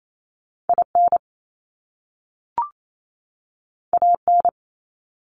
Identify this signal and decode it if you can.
Morse code, heard as a keyed tone, about 28 wpm, 700 Hz. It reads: IN AN
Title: Top Two Letter Combinations in Words 28wpm